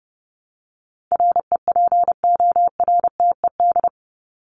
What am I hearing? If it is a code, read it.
Morse code, heard as a keyed tone, about 30 wpm, 700 Hz. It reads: REPORTED
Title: ED Suffix - Top 100 Words with Base Words 30wpm